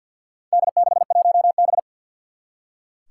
Morse code, heard as a keyed tone, about 46 wpm, 700 Hz. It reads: D61B
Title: Four Challenging Letter-Number Combinations 50wpm